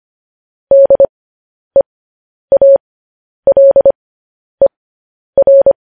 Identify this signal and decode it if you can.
Morse code, heard as a keyed tone, about 25 wpm, spaced as for 10 wpm, 550 Hz. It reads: DEALER
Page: News Headlines in Morse Code at 10 WPM